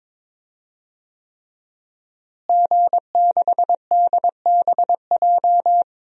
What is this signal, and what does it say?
Morse code, heard as a keyed tone, about 22 wpm, 700 Hz. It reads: G6DBJ
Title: DXCC-100 Call Signs - 5 Characters - Repeated 3x - 22wpm